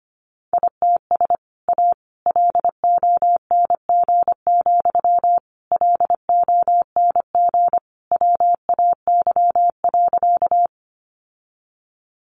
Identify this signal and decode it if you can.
Morse code, heard as a keyed tone, about 25 wpm, 700 Hz. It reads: ITS A LONG, LONG WAY.
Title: Sentences from Top 600 Words 25wpm